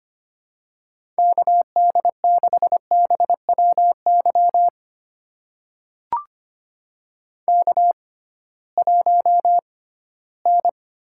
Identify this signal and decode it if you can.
Morse code, heard as a keyed tone, about 25 wpm, spaced as for 16 wpm, 700 Hz. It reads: KD6BWY K1N
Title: USA Call Signs - 6x Character Spacing - 25wpm